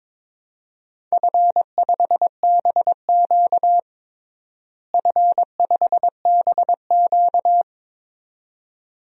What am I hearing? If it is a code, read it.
Morse code, heard as a keyed tone, about 22 wpm, 700 Hz. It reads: F5BQ F5BQ
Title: DXCC-100 Call Signs - 4 Characters - Repeated 2x - 22wpm